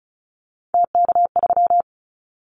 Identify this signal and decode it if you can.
Morse code, heard as a keyed tone, about 34 wpm, 700 Hz. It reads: TK3
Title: Callsign Prefixes for Top 100 DXCC - 35wpm